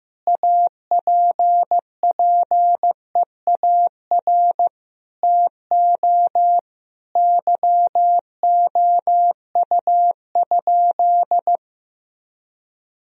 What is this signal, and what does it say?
Morse code, heard as a keyed tone, about 15 wpm, 700 Hz. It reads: APPEAR TO YOU?
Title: Sentences from Top 1000 Words - Review Part 2 - 15wpm